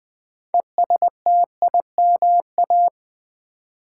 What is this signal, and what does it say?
Morse code, heard as a keyed tone, about 20 wpm, 700 Hz. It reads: ESTIMA
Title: Build A Word - 2 Letters At A Time - 8-Letter Words - 20wpm